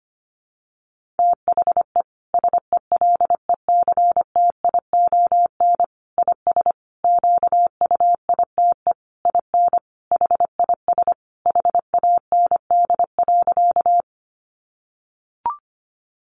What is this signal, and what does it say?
Morse code, heard as a keyed tone, about 25 wpm, 700 Hz. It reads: THE SELECTION IS QUITE IN HIS HAND.
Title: Sentences from Top 1400 Words 25wpm